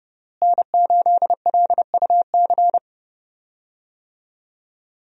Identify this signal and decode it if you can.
Morse code, heard as a keyed tone, about 30 wpm, 700 Hz. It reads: N8LUC
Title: USA Call Signs - 5 Characters - Repeated 2x - 30wpm